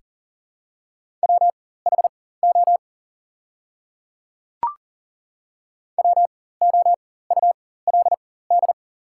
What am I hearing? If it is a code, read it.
Morse code, heard as a keyed tone, about 40 wpm, spaced as for 19 wpm, 700 Hz. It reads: WHO WOULD
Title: 4x Character Spacing - Top 100 Words - 40wpm